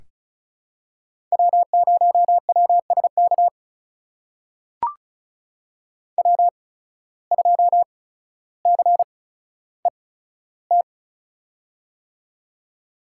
Morse code, heard as a keyed tone, about 36 wpm, spaced as for 15 wpm, 700 Hz. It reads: W0WSK W2CET
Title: USA Call Signs - 5 Characters - 8x Character Spacing - 35wpm